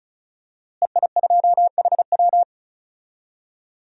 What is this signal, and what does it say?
Morse code, heard as a keyed tone, about 36 wpm, 700 Hz. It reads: EI2HW
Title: DXCC-100 Call Signs - 5 Characters - Repeated 3x - 35wpm